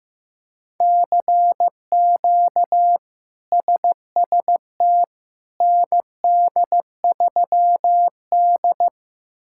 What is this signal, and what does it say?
Morse code, heard as a keyed tone, about 15 wpm, 700 Hz. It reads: CQ SST ND3D
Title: K1USN SST QSO Practice - 15wpm